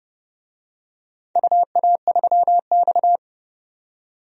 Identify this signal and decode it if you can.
Morse code, heard as a keyed tone, about 29 wpm, 700 Hz. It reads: UA3X
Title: DXCC-100 Call Signs - 4 Characters - 30wpm